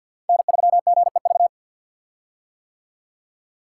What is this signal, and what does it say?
Morse code, heard as a keyed tone, about 48 wpm, 700 Hz. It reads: N3GEV
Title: Code Groups - 5 Letter:Numbers - 50wpm